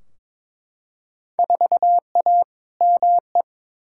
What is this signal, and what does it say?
Morse code, heard as a keyed tone, about 22 wpm, 700 Hz. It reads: 4A ME
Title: Field Day Exchanges - 22wpm